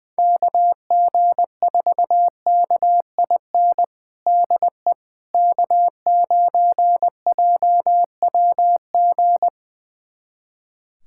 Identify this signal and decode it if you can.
Morse code, heard as a keyed tone, about 21 wpm, 700 Hz. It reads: KG4KIN DE K9JWG
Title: USA Callsign DE Callsign 20wpm